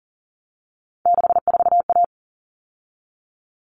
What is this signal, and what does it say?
Morse code, heard as a keyed tone, about 40 wpm, 700 Hz. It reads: 64A
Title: All Cut Numbers - 3-Digit 40wpm